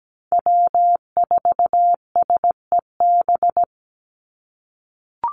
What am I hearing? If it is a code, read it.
Morse code, heard as a keyed tone, about 17 wpm, 700 Hz. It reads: W4SEB
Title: USA Call Signs 17wpm